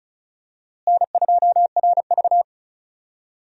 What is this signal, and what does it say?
Morse code, heard as a keyed tone, about 36 wpm, 700 Hz. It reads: N2RV
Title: USA Call Signs - 4 Characters - Repeated 3x - 35wpm